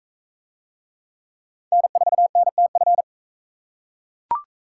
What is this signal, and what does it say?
Morse code, heard as a keyed tone, about 41 wpm, 700 Hz. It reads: N4NTF